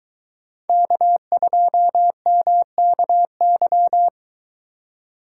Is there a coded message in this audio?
Yes: Morse code from 0.7 to 4.1 s, heard as a keyed tone, about 23 wpm, 700 Hz. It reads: K2MKY